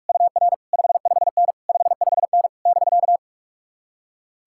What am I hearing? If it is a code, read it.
Morse code, heard as a keyed tone, about 46 wpm, 700 Hz. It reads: UR 55N 55N <BK>
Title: Speed-Racing Common QSO Phrases - 45wpm